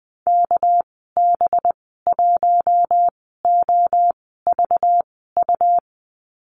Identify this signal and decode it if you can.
Morse code, heard as a keyed tone, about 20 wpm, spaced as for 15 wpm, 700 Hz. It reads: KB1OVU
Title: 231 Lesson 37g - Letter x - Call Signs 20wpm